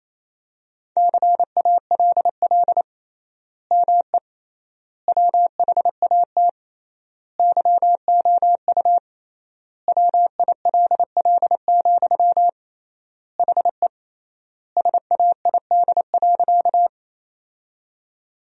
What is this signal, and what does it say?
Morse code, heard as a keyed tone, about 28 wpm, 700 Hz. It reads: CALL ME WHAT YOU WILL, HE SAID.
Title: 3x Word-Spacing - Sentences from Top 200 Words - 28wpm